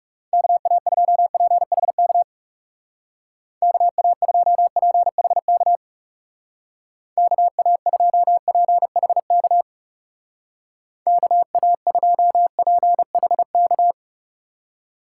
Morse code, heard as a keyed tone, about 37 wpm, 700 Hz. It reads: KA2PHK KA2PHK KA2PHK KA2PHK
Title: Speed Racing - USA 2x3 Call Signs 30wpm